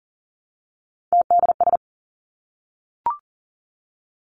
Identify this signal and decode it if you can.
Morse code, heard as a keyed tone, about 40 wpm, 700 Hz. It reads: TDS
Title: Three Challenging Letter-Number Combinations 40wpm